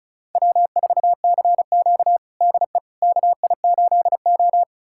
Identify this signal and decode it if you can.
Morse code, heard as a keyed tone, about 36 wpm, 700 Hz. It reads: W4CQ DE KI8O
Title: USA Callsign DE Callsign - 4 Characters - 35wpm